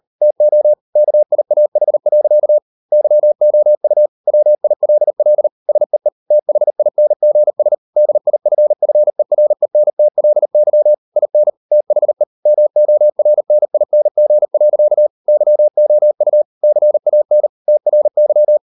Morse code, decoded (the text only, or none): TO KIAH. YOU WILL SEE THINGS DIFFERENTLY IN THE MORNING. YOU CAN TRY